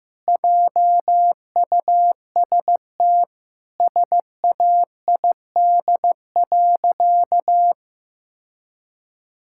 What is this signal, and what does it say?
Morse code, heard as a keyed tone, about 16 wpm, 700 Hz. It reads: JUST SAID.
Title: Sentences from Top 1600 Words 15wpm